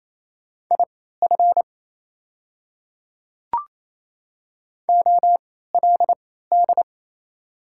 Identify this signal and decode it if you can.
Morse code, heard as a keyed tone, about 28 wpm, spaced as for 16 wpm, 700 Hz. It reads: IF OLD
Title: 3x Character Spacing - Top 200 Words - 28wpm